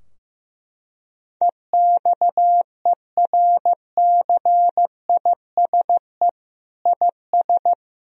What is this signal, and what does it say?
Morse code, heard as a keyed tone, about 15 wpm, 700 Hz. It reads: EXERCISE IS
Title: Sentences from Top 1000 Words - Review Part 4 - 15wpm